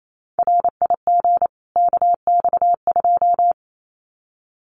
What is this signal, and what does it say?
Morse code, heard as a keyed tone, about 28 wpm, 700 Hz. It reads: RIG KX2
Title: Common QSO Phrases 28wpm